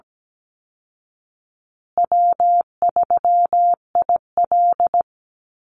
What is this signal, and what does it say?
Morse code, heard as a keyed tone, about 17 wpm, 700 Hz. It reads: W3IL
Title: Easy Call Signs Repeated 3x - 17wpm